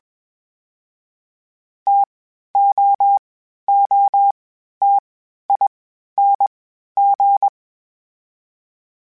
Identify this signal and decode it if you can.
Morse code, heard as a keyed tone, about 21 wpm, 800 Hz. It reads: T O O T I N G
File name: Variable Spacing / Variable Spacing-mp3s / TOs @20